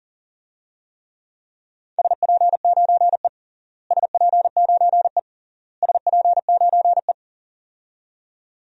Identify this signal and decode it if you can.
Morse code, heard as a keyed tone, about 40 wpm, 700 Hz. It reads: SP9E SP9E SP9E
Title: DXCC-100 Call Signs - 4 Characters - Repeated 3x - 40wpm